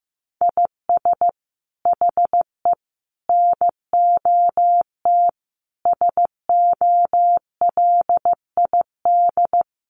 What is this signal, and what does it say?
Morse code, heard as a keyed tone, about 15 wpm, 700 Hz. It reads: IS HE NOT SOLID
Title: Sentences from Top 1500 Words 15wpm